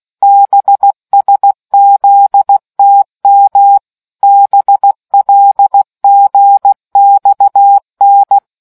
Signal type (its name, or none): Morse code